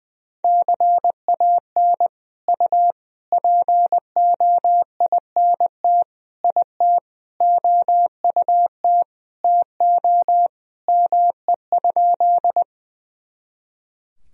Morse code, heard as a keyed tone, about 20 wpm, 700 Hz. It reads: CAN U POINT IT OUT TO ME?
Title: Sentences from Top 200 Words and Abbreviations 20wpm